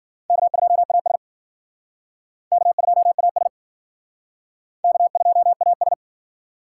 Morse code, heard as a keyed tone, about 51 wpm, 700 Hz. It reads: K2AS K2AS K2AS